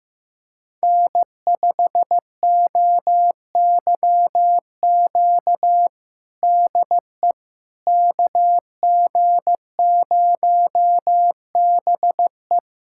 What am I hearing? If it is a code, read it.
Morse code, heard as a keyed tone, about 15 wpm, 700 Hz. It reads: N5OYQ DE KG0BE
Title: USA Callsign DE Callsign 15wpm